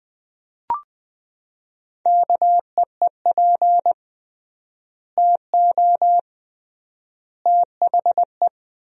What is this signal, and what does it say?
Morse code, heard as a keyed tone, about 20 wpm, 700 Hz. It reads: KEEP TO THE
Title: 3x Word-Spacing - Sentences from Top 300 Words - 20wpm